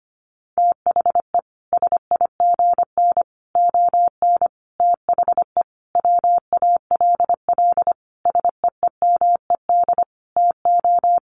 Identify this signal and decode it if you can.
Morse code, heard as a keyed tone, about 25 wpm, 700 Hz. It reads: THE SIGN ON THE WALL SEEMED TO
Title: A Sound Of Thunder 25wpm